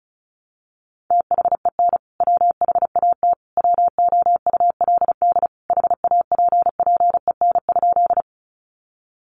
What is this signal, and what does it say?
Morse code, heard as a keyed tone, about 34 wpm, 700 Hz. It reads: THEN WHAT WOULD HAPPEN?
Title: Sentences from Top 400 Words 35wpm